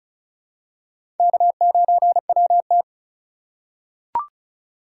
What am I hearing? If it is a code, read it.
Morse code, heard as a keyed tone, about 35 wpm, 700 Hz. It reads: K9WT
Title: USA Call Signs - 4 Characters - 3x Character Spacing - 35wpm